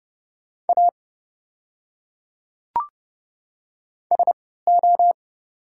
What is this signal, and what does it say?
Morse code, heard as a keyed tone, about 30 wpm, spaced as for 17 wpm, 690 Hz. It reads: A SO
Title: 3x Character Spacing - Top 200 Words - 30wpm